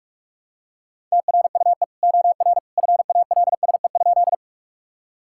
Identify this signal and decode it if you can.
Morse code, heard as a keyed tone, about 46 wpm, 700 Hz. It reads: TRUE OR FALSE?
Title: Sentences from Top 1600 Words 45wpm